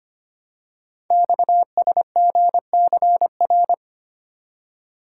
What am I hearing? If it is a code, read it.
Morse code, heard as a keyed tone, about 25 wpm, 690 Hz. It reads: XSGCR
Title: Code Groups - 5 Letters - 25wpm